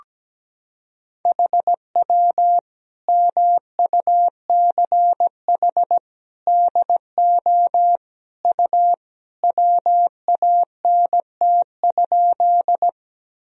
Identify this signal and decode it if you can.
Morse code, heard as a keyed tone, about 17 wpm, 700 Hz. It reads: HW MUCH DO U WANT?
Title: Sentences from Top 200 Words and Abbreviations 17wpm